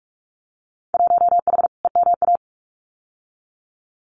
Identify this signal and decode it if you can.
Morse code, heard as a keyed tone, about 46 wpm, 700 Hz. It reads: 1H EMA
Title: Winter Field Day Exchanges 45wpm